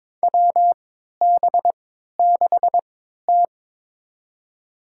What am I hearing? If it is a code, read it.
Morse code, heard as a keyed tone, about 22 wpm, spaced as for 13 wpm, 700 Hz. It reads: WB6T
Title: USA Call Signs - 3x Character Spacing - 22wpm